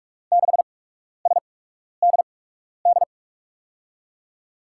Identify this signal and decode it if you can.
Morse code, heard as a keyed tone, about 46 wpm, spaced as for 12 wpm, 700 Hz. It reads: 6SDD
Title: USA Call Signs - 6 Characters - 8x Character Spacing - 45wpm